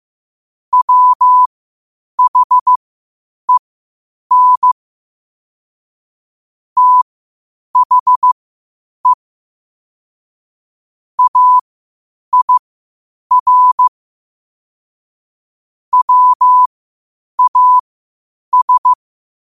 Morse code, heard as a keyed tone, about 15 wpm, spaced as for 8 wpm, 1000 Hz. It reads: WHEN THE AIR WAS
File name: qotdfiles / 8wpm